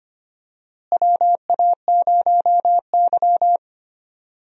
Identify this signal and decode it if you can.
Morse code, heard as a keyed tone, about 25 wpm, 700 Hz. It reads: WA0Y
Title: USA Call Signs - 4 Characters - Repeated 3x - 25wpm